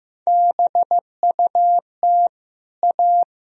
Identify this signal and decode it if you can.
Morse code, heard as a keyed tone, about 15 wpm, 700 Hz. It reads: BUT A
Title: Sentences from Top 1000 Words - Review Part 3 - 15wpm